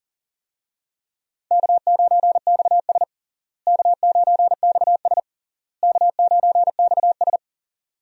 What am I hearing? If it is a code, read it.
Morse code, heard as a keyed tone, about 37 wpm, 700 Hz. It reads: K9XS K9XS K9XS
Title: USA Call Signs - 4 Characters - Repeated 3x - 40wpm